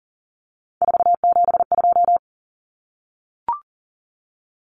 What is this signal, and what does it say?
Morse code, heard as a keyed tone, about 40 wpm, 700 Hz. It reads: 472
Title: Numbers - 3-Digit 40wpm